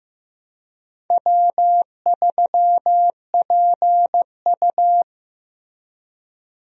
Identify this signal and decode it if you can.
Morse code, heard as a keyed tone, about 15 wpm, 700 Hz. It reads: W3PU
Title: USA Call Signs - 4 Characters - Repeated 2x - 15wpm